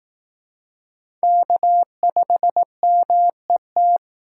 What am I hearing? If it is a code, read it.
Morse code, heard as a keyed tone, about 18 wpm, 700 Hz. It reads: K5MET
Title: Speed Racing - USA Call Signs 15wpm